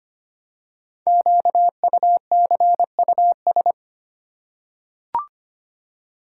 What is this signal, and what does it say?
Morse code, heard as a keyed tone, about 25 wpm, 700 Hz. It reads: QUCUS
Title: Code Groups - 5 Letters - 25wpm